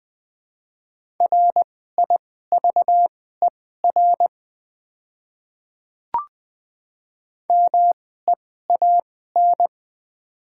Morse code, heard as a keyed tone, about 20 wpm, spaced as for 14 wpm, 700 Hz. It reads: RIVER MEAN
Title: 2x Character Spacing - Top 500 Words - 20wpm